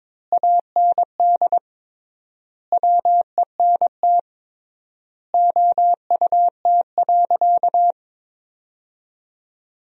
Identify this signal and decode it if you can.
Morse code, heard as a keyed tone, about 22 wpm, 700 Hz. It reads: AND WENT OUT.
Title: 3x Word-Spacing - Sentences from Top 300 Words - 22wpm